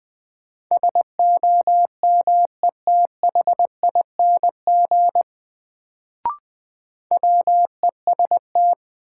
Morse code, heard as a keyed tone, about 20 wpm, 700 Hz. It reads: SOMETHING WEST
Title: Sets of 5 Words - New Word Drill - 20wpm